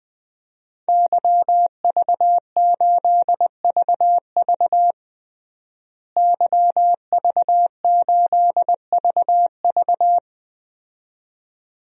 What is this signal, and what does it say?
Morse code, heard as a keyed tone, about 20 wpm, 700 Hz. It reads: YV8VV YV8VV
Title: DXCC-100 Call Signs - 5 Characters - Repeated 2x - 20wpm